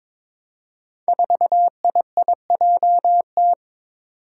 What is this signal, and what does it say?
Morse code, heard as a keyed tone, about 21 wpm, 700 Hz. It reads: 4IIJT
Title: Five Challenging Letter-Number Combinations 22wpm